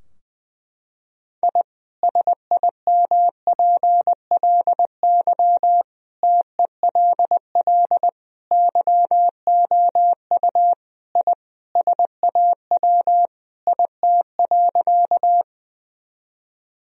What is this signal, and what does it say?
Morse code, heard as a keyed tone, about 20 wpm, 700 Hz. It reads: I SIMPLY TELL YOU I SAW IT.
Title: Sentences from Top 1000 Words - Review Part 4 - 20wpm